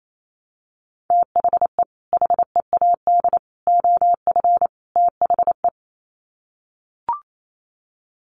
Morse code, heard as a keyed tone, about 28 wpm, 700 Hz. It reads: THE HEAD OF THE
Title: Sets of 4 Words 28wpm